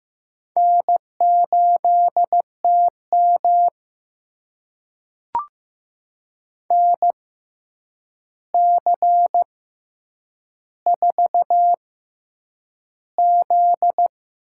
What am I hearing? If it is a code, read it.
Morse code, heard as a keyed tone, about 15 wpm, spaced as for 8 wpm, 700 Hz. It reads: N8TM NC4Z
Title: USA Call Signs - 4 Characters - 6x Character Spacing - 15wpm